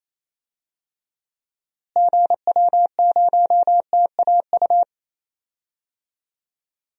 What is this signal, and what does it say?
Morse code, heard as a keyed tone, about 28 wpm, 700 Hz. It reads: GW0TAU
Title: DXCC-100 Call Signs - 6 Characters - Repeated 3x - 28wpm